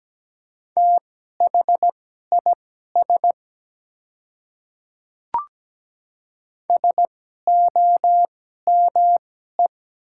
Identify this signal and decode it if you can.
Morse code, heard as a keyed tone, about 17 wpm, spaced as for 12 wpm, 700 Hz. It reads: THIS SOME
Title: 2x Character Spacing - Top 100 Words - 17wpm